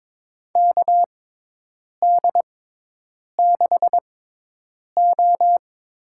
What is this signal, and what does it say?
Morse code, heard as a keyed tone, about 22 wpm, spaced as for 8 wpm, 700 Hz. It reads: KD6O
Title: USA Call Signs - 6 Characters - 6x Character Spacing - 22wpm